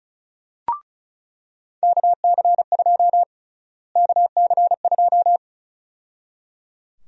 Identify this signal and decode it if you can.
Morse code, heard as a keyed tone, about 34 wpm, 700 Hz. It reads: KC2 KC2
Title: USA Call Sign Prefixes - Repeated 2x - 35wpm